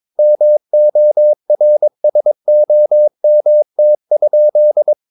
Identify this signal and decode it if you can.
Morse code, heard as a keyed tone, about 22 wpm, 600 Hz. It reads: MORSOMT?
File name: Ferie QRS 22wpm